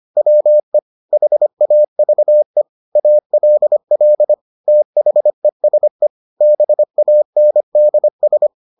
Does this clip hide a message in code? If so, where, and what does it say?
Yes: Morse code from 0.2 to 8.5 s, heard as a keyed tone, about 25 wpm, 600 Hz. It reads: WE HAVE ALL THESE BANDS